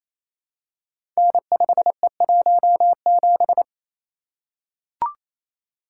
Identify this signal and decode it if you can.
Morse code, heard as a keyed tone, about 27 wpm, 700 Hz. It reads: N5E17
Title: Code Groups - 5 Letter:Numbers - 28wpm